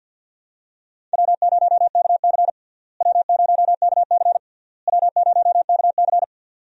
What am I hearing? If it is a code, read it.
Morse code, heard as a keyed tone, about 52 wpm, 700 Hz. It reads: W0KC W0KC W0KC
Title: USA Call Signs - 4 Characters - Repeated 3x - 50wpm